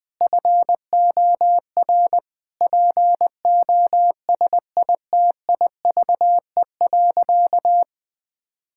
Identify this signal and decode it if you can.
Morse code, heard as a keyed tone, about 21 wpm, 700 Hz. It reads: FOR POSITIVE.
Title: Sentences from Top 1000 Words - Review Part 5 - 20wpm